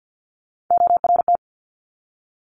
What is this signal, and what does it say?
Morse code, heard as a keyed tone, about 50 wpm, 700 Hz. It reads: ORT
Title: Top Three Letter Combinations in Words 50wpm